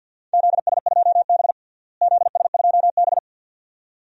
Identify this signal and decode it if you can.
Morse code, heard as a keyed tone, about 50 wpm, 700 Hz. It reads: ZS2B ZS2B
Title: DXCC-100 Call Signs - 4 Characters - Repeated 2x - 50wpm